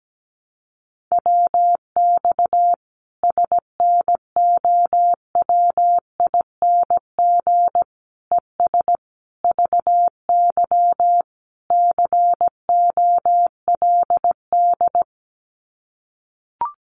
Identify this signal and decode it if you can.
Morse code, heard as a keyed tone, about 17 wpm, 700 Hz. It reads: WX SNOWING ES VY COLD
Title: Common QSO Phrases 17wpm